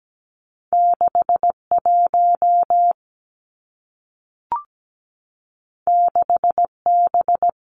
Morse code, heard as a keyed tone, about 17 wpm, 700 Hz. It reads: 6 1 6 B
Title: Two Challenging Letter-Number Combinations 17wpm